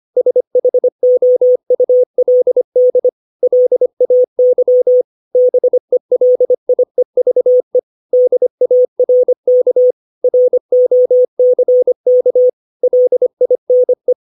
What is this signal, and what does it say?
Morse code, heard as a keyed tone, about 25 wpm, 500 Hz. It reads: SHOULD LAY BELIEVE DARK ROCK LINE
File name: 632 words / 632 Words 25WPM